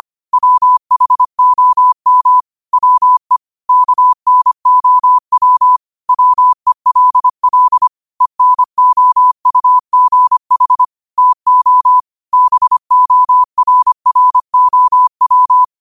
Morse code, heard as a keyed tone, about 25 wpm, 1000 Hz. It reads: WHOM WE KNOW WELL ENOUGH TO BORROW
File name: qotdfiles / 25wpm